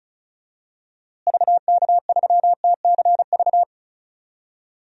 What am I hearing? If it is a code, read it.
Morse code, heard as a keyed tone, about 36 wpm, 700 Hz. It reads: VK3TCV
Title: DXCC-100 Call Signs - 6 Characters - Repeated 2x - 35wpm